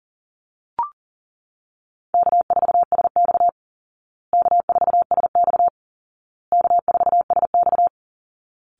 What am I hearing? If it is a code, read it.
Morse code, heard as a keyed tone, about 40 wpm, 700 Hz. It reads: K4SX K4SX K4SX
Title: Call Signs Repeated 2x - 40wpm